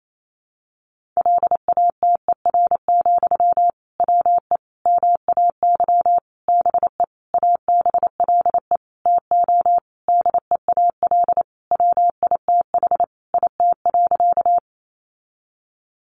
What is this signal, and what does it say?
Morse code, heard as a keyed tone, about 28 wpm, 700 Hz. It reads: LATER, WE MAY BE ABLE TO DEAL WITH IT.
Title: Sentences from Top 800 Words 28wpm